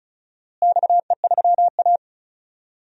Morse code, heard as a keyed tone, about 36 wpm, 700 Hz. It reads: XE3A
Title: DXCC-100 Call Signs - 4 Characters - Repeated 2x - 35wpm